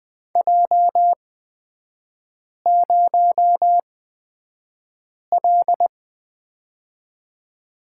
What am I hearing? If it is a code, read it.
Morse code, heard as a keyed tone, about 20 wpm, spaced as for 5 wpm, 700 Hz. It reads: J0L
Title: 243 Lesson 39e - Letter z - Three Characters 20wpm